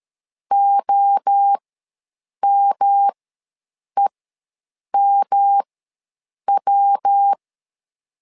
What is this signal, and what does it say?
Morse code, heard as a keyed tone, about 13 wpm, spaced as for 7 wpm, 800 Hz. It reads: OMEMW
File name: LEZIONI / 3 MWRO / 3 MWRO ws 12wpm 1914